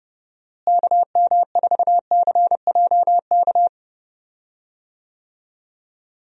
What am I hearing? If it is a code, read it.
Morse code, heard as a keyed tone, about 30 wpm, 700 Hz. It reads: KM4CJK
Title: USA Call Signs - 6 Characters - Repeated 3x - 30wpm